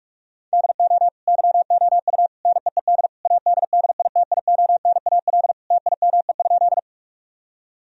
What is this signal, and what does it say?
Morse code, heard as a keyed tone, about 46 wpm, 700 Hz. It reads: DO YOU NEED ADDITIONAL TIME?
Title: Sentences from Top 1000 Words - Review Part 4 - 45wpm